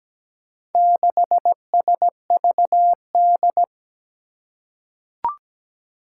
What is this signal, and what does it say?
Morse code, heard as a keyed tone, about 17 wpm, 700 Hz. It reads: 6SVD
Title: Four Challenging Letter-Number Combinations 17wpm